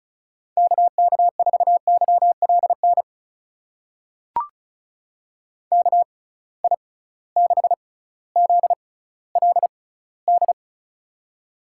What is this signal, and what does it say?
Morse code, heard as a keyed tone, about 36 wpm, spaced as for 18 wpm, 700 Hz. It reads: KK4YLN KI6ZLD
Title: USA Call Signs - 6 Characters - 6x Character Spacing - 35wpm